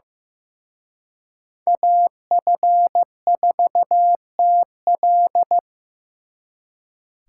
Morse code, heard as a keyed tone, about 18 wpm, 700 Hz. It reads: AF4TL
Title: Speed Racing - USA 2x2 Call Signs 15wpm